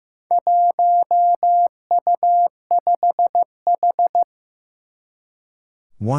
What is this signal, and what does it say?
Morse code, heard as a keyed tone, about 15 wpm, 700 Hz. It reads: 1U5H
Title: Four Challenging Letter-Number Combinations 15wpm